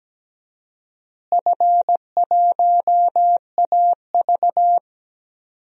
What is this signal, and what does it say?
Morse code, heard as a keyed tone, about 17 wpm, 700 Hz. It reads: F1AV
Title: DXCC-100 Call Signs - 4 Characters - Repeated 3x - 17wpm